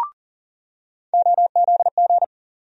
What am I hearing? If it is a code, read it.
Morse code, heard as a keyed tone, about 40 wpm, 700 Hz. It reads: OZG
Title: USA Call Sign Postfixes - 3 Letters 40wpm